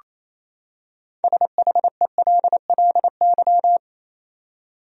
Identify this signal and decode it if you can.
Morse code, heard as a keyed tone, about 28 wpm, 700 Hz. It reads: SHELLY
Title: USA Common Names - 28wpm